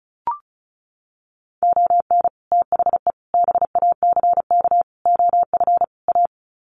Morse code, heard as a keyed tone, about 34 wpm, 700 Hz. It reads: ON THE BACK OF A
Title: Sets of 5 Words Encore2 35wpm